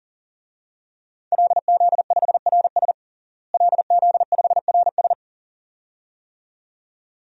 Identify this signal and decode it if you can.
Morse code, heard as a keyed tone, about 40 wpm, 700 Hz. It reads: LZ5RS LZ5RS
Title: DXCC-100 Call Signs - 5 Characters - Repeated 2x - 40wpm